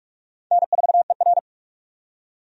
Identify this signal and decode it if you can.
Morse code, heard as a keyed tone, about 42 wpm, 700 Hz. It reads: N4ER
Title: USA Call Signs - 4 Characters - 45wpm